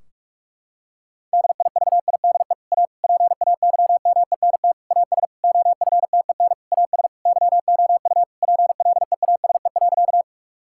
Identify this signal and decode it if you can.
Morse code, heard as a keyed tone, about 46 wpm, 700 Hz. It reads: DIVIDE A PAYMENT AS OFTEN AS YOU PLEASE.
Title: Sentences from Top 1600 Words 45wpm